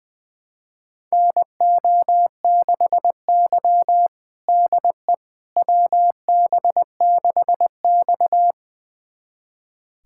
Morse code, heard as a keyed tone, about 20 wpm, 700 Hz. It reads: NO6Y DE WB6X
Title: USA Callsign DE Callsign - 4 Characters - 20wpm